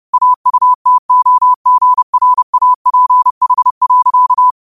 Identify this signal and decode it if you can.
Morse code, heard as a keyed tone, about 31 wpm, 1000 Hz. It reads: AUTOGRAPH.
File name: qotdfiles / 30wpm